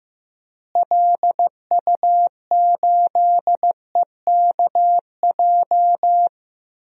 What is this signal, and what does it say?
Morse code, heard as a keyed tone, about 15 wpm, 700 Hz. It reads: LU8EKJ
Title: DXCC-100 Call Signs - 6 Characters - Repeated 3x - 15wpm